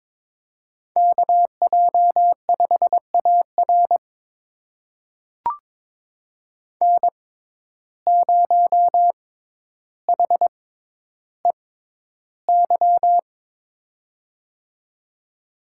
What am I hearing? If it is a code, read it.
Morse code, heard as a keyed tone, about 22 wpm, spaced as for 11 wpm, 700 Hz. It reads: KJ5AR N0HEY
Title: USA Call Signs - 5 Characters - 6x Character Spacing - 22wpm